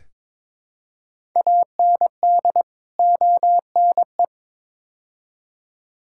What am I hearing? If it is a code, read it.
Morse code, heard as a keyed tone, about 22 wpm, 700 Hz. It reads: AND ONE
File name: practice / 22wpm / Build A Phrase - 5-Word Phrases - 22wpm